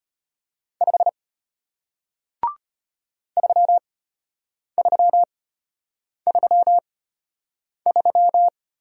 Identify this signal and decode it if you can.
Morse code, heard as a keyed tone, about 33 wpm, spaced as for 8 wpm, 700 Hz. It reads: 5 3333